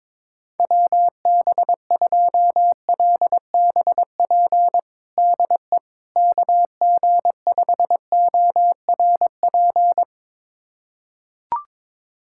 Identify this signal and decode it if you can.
Morse code, heard as a keyed tone, about 22 wpm, 700 Hz. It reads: WB2LBP DE KG5ORP
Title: USA Callsign DE Callsign - 6 Characters - 22wpm